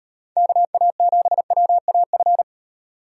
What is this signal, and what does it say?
Morse code, heard as a keyed tone, about 38 wpm, 700 Hz. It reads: KA7WAF